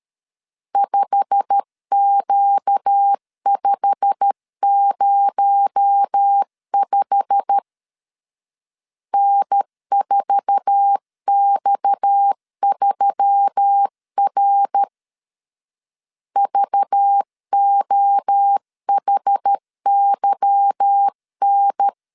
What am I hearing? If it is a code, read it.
Morse code, heard as a keyed tone, about 13 wpm, 800 Hz. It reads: 5Q505 N4X3R VOHYN
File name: LEZIONI / 8 YJ45 / 8 YJ45 12wpm 2151